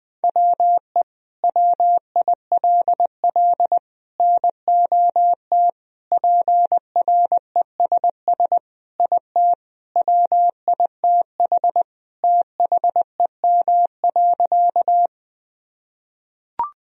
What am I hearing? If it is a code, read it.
Morse code, heard as a keyed tone, about 20 wpm, 700 Hz. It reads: WE WILL NOT PRESS IT WITH THEM.
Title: Sentences from Top 1000 Words - Review Part 1 - 20wpm